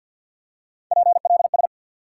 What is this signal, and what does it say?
Morse code, heard as a keyed tone, about 53 wpm, 700 Hz. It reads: PLS
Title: Top 90 QSO Elements 50wpm